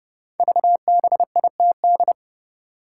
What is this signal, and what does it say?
Morse code, heard as a keyed tone, about 30 wpm, 700 Hz. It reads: VBITD
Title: Code Groups - 5 Letters - 30wpm